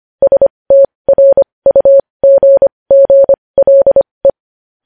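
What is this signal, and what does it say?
Morse code, heard as a keyed tone, about 25 wpm, spaced as for 20 wpm, 550 Hz. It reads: STRUGGLE
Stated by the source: News headlines sent in Morse code at 20.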